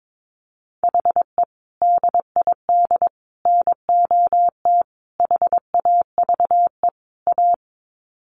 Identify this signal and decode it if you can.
Morse code, heard as a keyed tone, about 22 wpm, 700 Hz. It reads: HE DID NOT HAVE A
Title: Sets of 5 Words Encore 22wpm